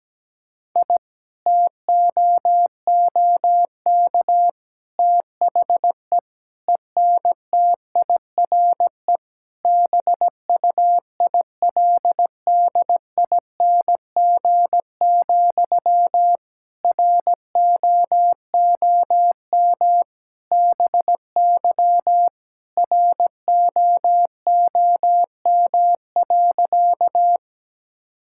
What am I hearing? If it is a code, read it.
Morse code, heard as a keyed tone, about 17 wpm, 700 Hz. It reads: I TOOK THE ENTIRE BUILDING, ROOM BY ROOM.
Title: Sentences from Top 1000 Words - Review Part 5 - 17wpm